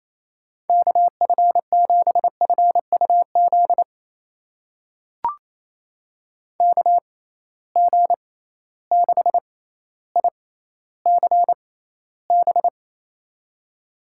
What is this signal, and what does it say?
Morse code, heard as a keyed tone, about 28 wpm, spaced as for 14 wpm, 700 Hz. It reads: KF7FUZ KG6ICB
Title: USA Call Signs - 6 Characters - 6x Character Spacing - 28wpm